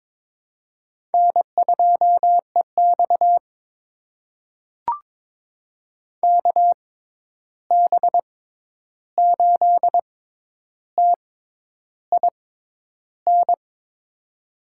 Morse code, heard as a keyed tone, about 22 wpm, spaced as for 10 wpm, 700 Hz. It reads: N2EX KB8TIN
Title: USA Call Signs - 6x Character Spacing - 22wpm